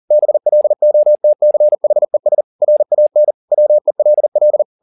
Morse code, heard as a keyed tone, about 40 wpm, 600 Hz. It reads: BLOTCHES RAN WELL